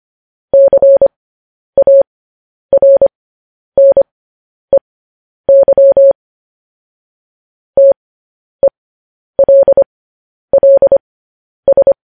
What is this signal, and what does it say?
Morse code, heard as a keyed tone, about 25 wpm, spaced as for 10 wpm, 550 Hz. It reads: CARNEY TELLS
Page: news-headlines-in-morse-code-at-10-wpm